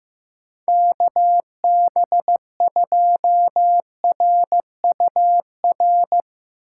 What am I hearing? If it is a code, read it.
Morse code, heard as a keyed tone, about 15 wpm, 700 Hz. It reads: KB2RUR
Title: USA Call Signs - 3x Character Spacing - 15wpm